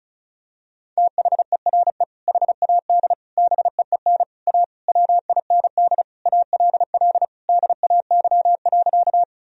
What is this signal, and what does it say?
Morse code, heard as a keyed tone, about 36 wpm, 700 Hz. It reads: THERE HAD BEEN A WIND ALL DAY.
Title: Sentences from Top 1000 Words - Review Part 2 - 35wpm